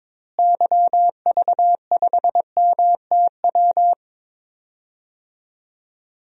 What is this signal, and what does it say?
Morse code, heard as a keyed tone, about 22 wpm, 700 Hz. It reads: YV5MTW
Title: DXCC-100 Call Signs - 6 Characters - Repeated 2x - 22wpm